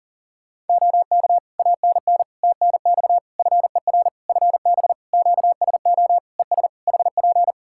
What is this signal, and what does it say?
Morse code, heard as a keyed tone, about 40 wpm, 700 Hz. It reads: OK ANN TNX FER FB QSO ES HP
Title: Practice QSO - Exchange 4 - Ending - 40wpm